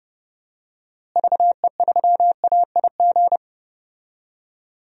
Morse code, heard as a keyed tone, about 32 wpm, 700 Hz. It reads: VE3AIG
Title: DXCC-100 Call Signs - 6 Characters - Repeated 3x - 30wpm